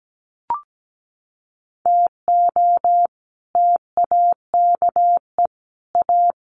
Morse code, heard as a keyed tone, about 17 wpm, 700 Hz. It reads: TO TAKE A
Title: Sets of 5 Words 17wpm